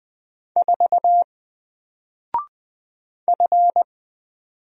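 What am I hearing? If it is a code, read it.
Morse code, heard as a keyed tone, about 20 wpm, 700 Hz. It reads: 4 F
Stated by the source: Single letter and number Morse drill at 20 wpm, from N4LCD.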